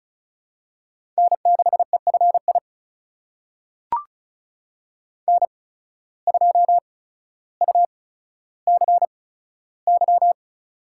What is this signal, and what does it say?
Morse code, heard as a keyed tone, about 36 wpm, spaced as for 15 wpm, 700 Hz. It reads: N6EFI N2UCY
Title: USA Call Signs - 5 Characters - 8x Character Spacing - 35wpm